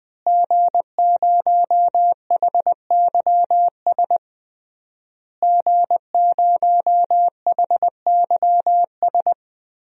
Morse code, heard as a keyed tone, about 20 wpm, 700 Hz. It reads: G0HYS G0HYS
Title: DXCC-100 Call Signs - 5 Characters - Repeated 3x - 20wpm